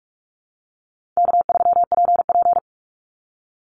Morse code, heard as a keyed tone, about 44 wpm, 700 Hz. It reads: K3PP
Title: Easy Call Signs - 45wpm